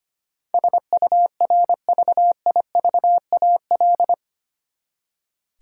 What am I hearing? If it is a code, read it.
Morse code, heard as a keyed tone, about 25 wpm, 700 Hz. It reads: SURVIVAL
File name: Eight Letter Words - 25wpm